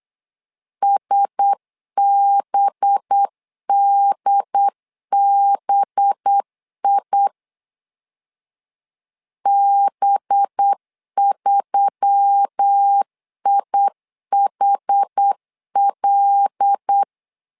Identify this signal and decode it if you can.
Morse code, heard as a keyed tone, about 8 wpm, 800 Hz. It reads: SBDBI B3IHL
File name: LEZIONI / 5 LBVC / 5 LB 8wpm 2816